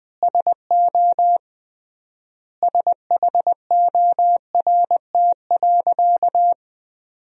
Morse code, heard as a keyed tone, about 21 wpm, 700 Hz. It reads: SO SHORT.
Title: 3x Word-Spacing - Sentences from Top 500 Words - 20wpm